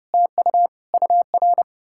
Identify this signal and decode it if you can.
Morse code, heard as a keyed tone, about 40 wpm, 700 Hz. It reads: TU UR